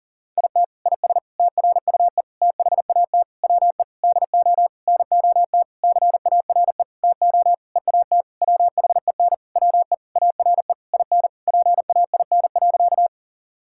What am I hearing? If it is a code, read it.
Morse code, heard as a keyed tone, about 40 wpm, 700 Hz. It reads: IT IS TRUE THAT WE DO NOT CARE TO EAT WHEN WE ARE IN PAIN.
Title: Sentences from Top 1600 Words 40wpm